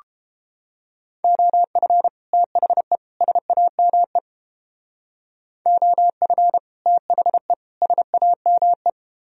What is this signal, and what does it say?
Morse code, heard as a keyed tone, about 29 wpm, 700 Hz. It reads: OF THE SAME OF THE SAME